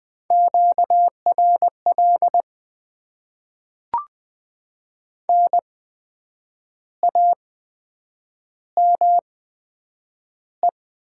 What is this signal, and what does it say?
Morse code, heard as a keyed tone, about 20 wpm, spaced as for 8 wpm, 700 Hz. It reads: QRL NAME
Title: Top 20 QSO Elements - 8x Character Spacing - 20wpm